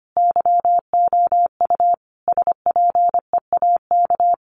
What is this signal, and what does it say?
Morse code, heard as a keyed tone, about 25 wpm, 700 Hz. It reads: YOU SPEAK
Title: Sentences from Top 1200 Words 25wpm